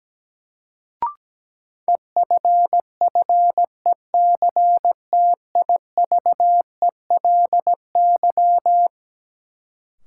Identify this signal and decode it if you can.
Morse code, heard as a keyed tone, about 17 wpm, 700 Hz. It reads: EFFECTIVELY
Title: Sentences from Top 1400 Words - New Word Drill - 17wpm